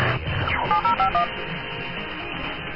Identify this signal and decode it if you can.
DTMF, heard as telephone touch tones, tones of about 110 ms. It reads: 7051
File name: sdr.ok2kyj.cz 1413.00 amn